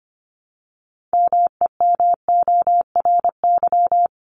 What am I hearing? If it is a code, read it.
Morse code, heard as a keyed tone, about 25 wpm, 700 Hz. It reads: MEMORY
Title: Top 1251-1500 Words 25wpm